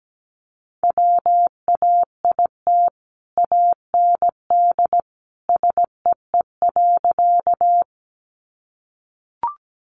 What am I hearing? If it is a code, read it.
Morse code, heard as a keyed tone, about 17 wpm, 700 Hz. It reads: WAIT AND SEE.
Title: Sentences from Top 500 Words 17wpm